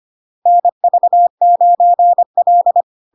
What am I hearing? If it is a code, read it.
Morse code, heard as a keyed tone, about 25 wpm, 700 Hz. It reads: NV9L
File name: Log-in-CW-25wpm-700-Hz